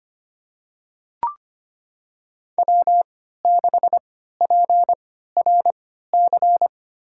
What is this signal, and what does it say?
Morse code, heard as a keyed tone, about 26 wpm, spaced as for 14 wpm, 700 Hz. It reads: W6PRC
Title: USA Call Signs - 5 Characters - 3x Character Spacing - 25wpm